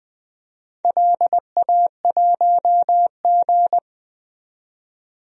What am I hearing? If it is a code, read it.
Morse code, heard as a keyed tone, about 20 wpm, 700 Hz. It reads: LA1G
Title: DXCC-100 Call Signs - 4 Characters - 20wpm